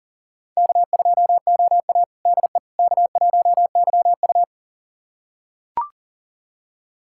Morse code, heard as a keyed tone, about 40 wpm, 700 Hz. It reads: K2OA DE K1YU
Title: USA Callsign DE Callsign - 4 Characters - 40wpm